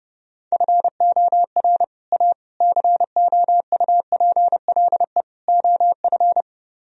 Morse code, heard as a keyed tone, about 30 wpm, 700 Hz. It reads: FOR A COUPLE OF